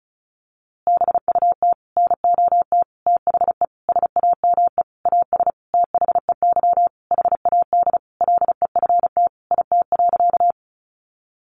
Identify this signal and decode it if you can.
Morse code, heard as a keyed tone, about 34 wpm, 700 Hz. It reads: BUT NOT THE SAME AS THEY HAD LEFT IT.
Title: A Sound Of Thunder 35wpm